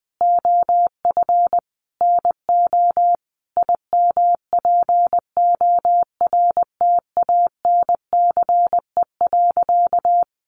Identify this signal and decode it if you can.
Morse code, heard as a keyed tone, about 20 wpm, 700 Hz. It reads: OF NO IMPORTANCE.
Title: Sentences from Top 1200 Words 20wpm